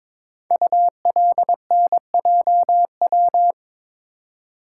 Morse code, heard as a keyed tone, about 22 wpm, 700 Hz. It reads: ULNJW
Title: Code Groups - 5 Letters - 22wpm